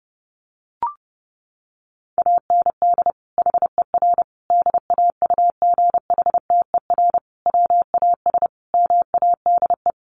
Morse code, heard as a keyed tone, about 30 wpm, 700 Hz. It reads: AND HER DAUGHTER WAS MADE
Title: Sentences from Top 1500 Words 30wpm